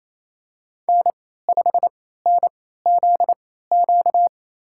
Morse code, heard as a keyed tone, about 28 wpm, spaced as for 16 wpm, 700 Hz. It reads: N5NZQ
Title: USA Call Signs - 3x Character Spacing - 28wpm